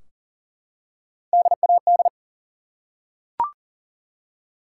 Morse code, heard as a keyed tone, about 42 wpm, 700 Hz. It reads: DAD
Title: Three Letter Words - 40wpm